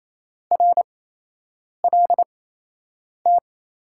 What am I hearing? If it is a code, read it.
Morse code, heard as a keyed tone, about 28 wpm, spaced as for 8 wpm, 700 Hz. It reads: RLT
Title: USA Call Signs - 6 Characters - 8x Character Spacing - 28wpm